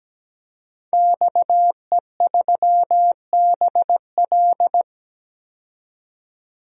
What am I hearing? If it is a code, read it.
Morse code, heard as a keyed tone, about 17 wpm, 700 Hz. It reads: XE3BL
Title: DXCC-100 Call Signs - 5 Characters - 17wpm